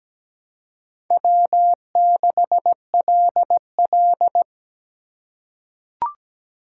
Morse code, heard as a keyed tone, about 17 wpm, 700 Hz. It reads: W6LL
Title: USA Call Signs - 4 Characters - 4x Character Spacing - 17wpm